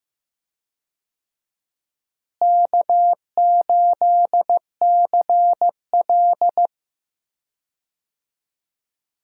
Morse code, heard as a keyed tone, about 16 wpm, 700 Hz. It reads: K8CL
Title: USA Call Signs - 4 Characters - Repeated 2x - 15wpm